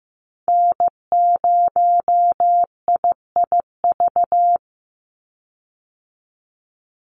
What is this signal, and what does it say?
Morse code, heard as a keyed tone, about 15 wpm, 700 Hz. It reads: N0IIV
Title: USA Call Signs 15wpm